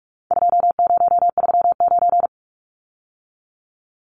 Morse code, heard as a keyed tone, about 44 wpm, 700 Hz. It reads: 2039
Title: Numbers - 4-Digit 45wpm